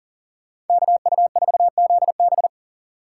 Morse code, heard as a keyed tone, about 49 wpm, 700 Hz. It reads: KU4ZB
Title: Speed Racing - USA 2x2 Call Signs 40wpm